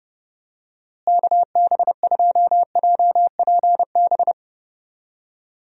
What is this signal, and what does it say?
Morse code, heard as a keyed tone, about 30 wpm, 700 Hz. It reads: KB2JPB
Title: USA Call Signs - 6 Characters - Repeated 3x - 30wpm